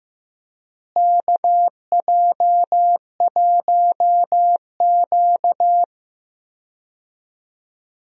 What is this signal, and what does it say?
Morse code, heard as a keyed tone, about 15 wpm, 700 Hz. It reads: KJ1Q
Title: USA Call Signs - 4 Characters - Repeated 2x - 15wpm